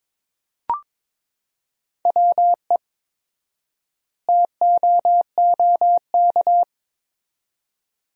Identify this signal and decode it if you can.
Morse code, heard as a keyed tone, about 22 wpm, 700 Hz. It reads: WE TOOK
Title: 4x Word-Spacing - Sentences from Top 400 Words - 22wpm